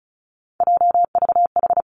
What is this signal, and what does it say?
Morse code, heard as a keyed tone, about 36 wpm, 700 Hz. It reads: JVH
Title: Three Challenging Letter-Number Combinations 35wpm